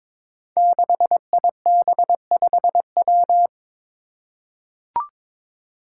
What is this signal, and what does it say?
Morse code, heard as a keyed tone, about 22 wpm, 700 Hz. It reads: 6IB5W
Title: Five Challenging Letter-Number Combinations 22wpm